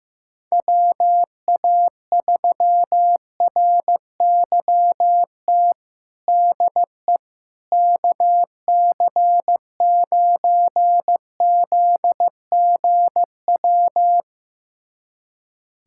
Morse code, heard as a keyed tone, about 15 wpm, 700 Hz. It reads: WA3RYT DE KC9ZGW
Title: USA Callsign DE Callsign 15wpm